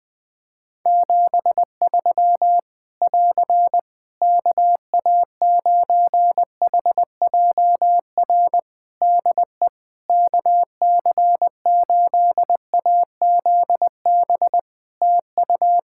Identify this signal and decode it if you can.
Morse code, heard as a keyed tone, about 21 wpm, 700 Hz. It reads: 73 <AR> KA9HJR DE KC8AZB TU
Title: Practice QSO - Exchange 4 - Ending - 20wpm